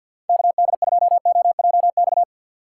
Morse code, heard as a keyed tone, about 52 wpm, 700 Hz. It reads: KD2OJX
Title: USA Call Signs - 6 Characters - 6x Character Spacing - 50wpm